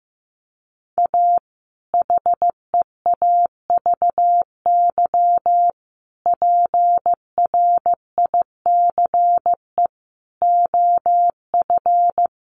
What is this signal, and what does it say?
Morse code, heard as a keyed tone, about 15 wpm, 700 Hz. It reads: A HEAVY PRICE OF
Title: Sentences from Top 1100 Words 15wpm